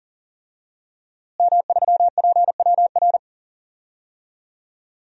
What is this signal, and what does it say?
Morse code, heard as a keyed tone, about 40 wpm, 700 Hz. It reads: M3PWR
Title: DXCC-100 Call Signs - 5 Characters - Repeated 2x - 40wpm